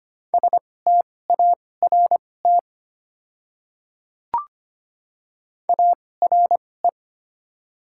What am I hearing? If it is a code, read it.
Morse code, heard as a keyed tone, about 25 wpm, spaced as for 18 wpm, 690 Hz. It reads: START ARE
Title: 2x Character Spacing - Top 300 Words - 25wpm